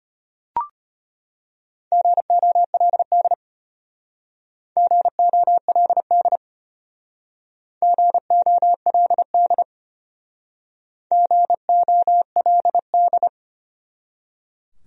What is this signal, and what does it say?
Morse code, heard as a keyed tone, about 32 wpm, 700 Hz. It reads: GOLD GOLD GOLD GOLD